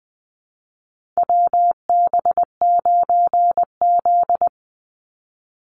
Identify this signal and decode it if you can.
Morse code, heard as a keyed tone, about 20 wpm, 700 Hz. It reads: WB9Z
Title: Call Signs Repeated 2x - 20wpm